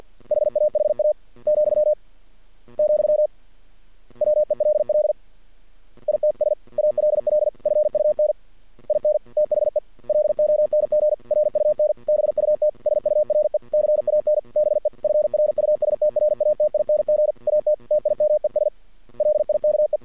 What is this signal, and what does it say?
Morse code, heard as a keyed tone, about 48 wpm, 600 Hz. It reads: HAHA , , WELL ITS ALL FUN IN THE LONG RUN BUT SURE CAN BE FRUSTRATING AT TIMES LIKE
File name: archived / CRSnet